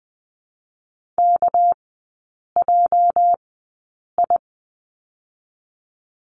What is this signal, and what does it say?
Morse code, heard as a keyed tone, about 20 wpm, spaced as for 8 wpm, 700 Hz. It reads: KJI
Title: 205 Lesson 33e - Letter j - Three Characters 20wpm